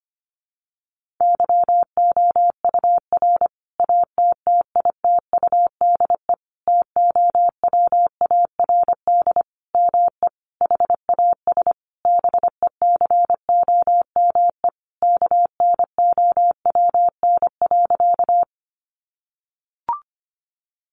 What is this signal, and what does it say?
Morse code, heard as a keyed tone, about 25 wpm, 700 Hz. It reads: YOUR ATTITUDE TOWARD ME HAS BECOME KNOWN.
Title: Sentences from Top 1300 Words 25wpm